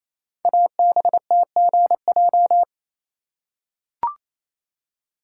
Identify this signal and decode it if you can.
Morse code, heard as a keyed tone, about 28 wpm, 700 Hz. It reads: ABTGJ
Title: Code Groups - 5 Letters - 28wpm